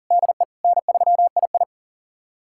Morse code, heard as a keyed tone, about 37 wpm, 700 Hz. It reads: DE N3II
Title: USA Callsign DE Callsign - 4 Characters - 40wpm